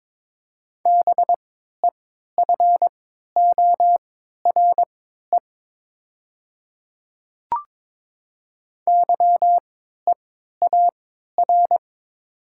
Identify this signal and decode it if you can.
Morse code, heard as a keyed tone, about 22 wpm, spaced as for 13 wpm, 700 Hz. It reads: BEFORE YEAR
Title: 3x Character Spacing - Top 200 Words - 22wpm